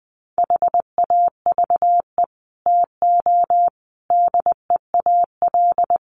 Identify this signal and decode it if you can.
Morse code, heard as a keyed tone, about 20 wpm, 700 Hz. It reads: HAVE TO DEAL
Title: Sets of 5 Words Encore2 20wpm